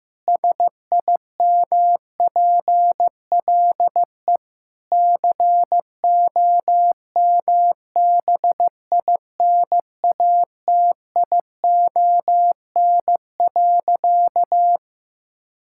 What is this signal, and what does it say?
Morse code, heard as a keyed tone, about 15 wpm, 700 Hz. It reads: SIMPLE COMBINATION.
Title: Sentences from Top 1000 Words - Review Part 6 - 15wpm